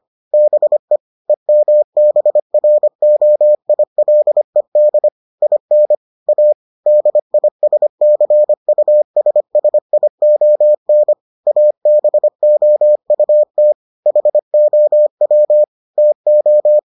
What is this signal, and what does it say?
Morse code, heard as a keyed tone, about 25 wpm, 600 Hz. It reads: BE EMBROILED IN A DISCUSSION ABOUT HOW TO